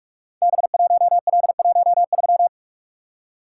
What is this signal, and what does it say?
Morse code, heard as a keyed tone, about 44 wpm, 700 Hz. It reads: B1L13
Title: Code Groups - 5 Letter:Numbers - 45wpm